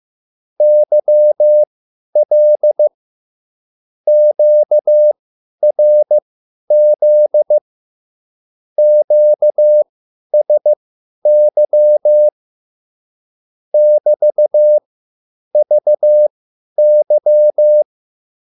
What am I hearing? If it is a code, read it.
Morse code, heard as a keyed tone, about 15 wpm, spaced as for 10 wpm, 600 Hz. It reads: YL QRZ QSY <BT>VY